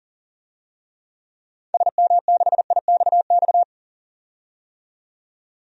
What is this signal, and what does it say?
Morse code, heard as a keyed tone, about 40 wpm, 700 Hz. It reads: SM6IXX
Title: DXCC-100 Call Signs - 6 Characters - 40wpm